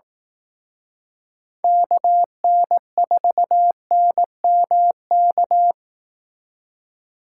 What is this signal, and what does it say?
Morse code, heard as a keyed tone, about 18 wpm, 700 Hz. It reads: KN4NMK
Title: Speed Racing - USA 2x3 Call Signs 15wpm